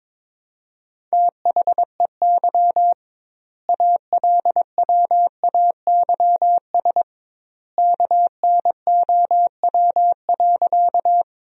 Morse code, heard as a keyed tone, about 22 wpm, 700 Hz. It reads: THEY ALWAYS KNOW.
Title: 2x Word-Spacing - Sentences from Top 300 Words - 22wpm